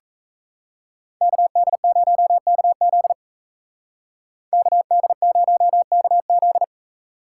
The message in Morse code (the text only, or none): KD0KZ KD0KZ